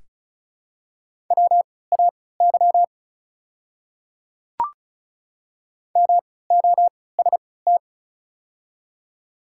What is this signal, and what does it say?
Morse code, heard as a keyed tone, about 36 wpm, spaced as for 20 wpm, 700 Hz. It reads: WAY MOST
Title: 3x Character Spacing - Top 100 Words - 35wpm